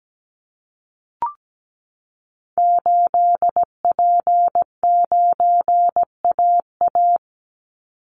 Morse code, heard as a keyed tone, about 17 wpm, 700 Hz. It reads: 8P9AA
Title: Call Signs Repeated 3x - 17wpm